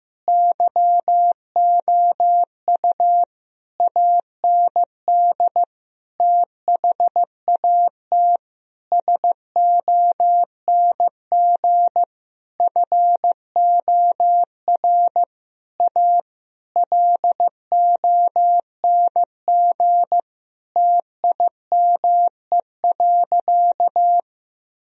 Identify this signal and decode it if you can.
Morse code, heard as a keyed tone, about 15 wpm, 700 Hz. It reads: YOU AND THAT SONG FOR A LONG TIME.
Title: Sentences from Top 1000 Words - Review Part 2 - 15wpm